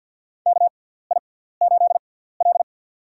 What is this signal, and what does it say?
Morse code, heard as a keyed tone, about 51 wpm, spaced as for 17 wpm, 700 Hz. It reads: KI8L
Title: USA Call Signs - 4 Characters - 6x Character Spacing - 50wpm